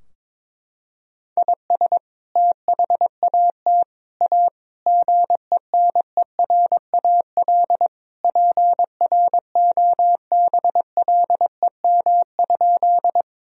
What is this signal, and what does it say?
Morse code, heard as a keyed tone, about 22 wpm, 700 Hz. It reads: IS THAT A GENERAL PROBLEM?
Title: Sentences from Top 1000 Words - Review Part 3 - 22wpm